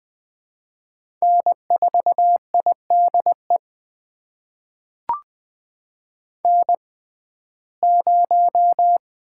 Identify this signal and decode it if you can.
Morse code, heard as a keyed tone, about 20 wpm, spaced as for 14 wpm, 700 Hz. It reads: N4IDE N0
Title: USA Call Signs - 5 Characters - 6x Character Spacing - 20wpm